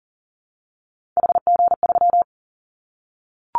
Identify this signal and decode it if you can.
Morse code, heard as a keyed tone, about 41 wpm, 700 Hz. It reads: HG3
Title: Callsign Prefixes for Top 25 DXCC - 40wpm